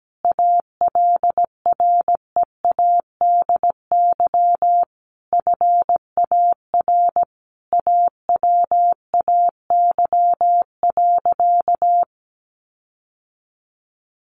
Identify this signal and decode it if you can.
Morse code, heard as a keyed tone, about 17 wpm, 700 Hz. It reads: ALREADY FAR AWAY.
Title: Sentences from Top 1500 Words 17wpm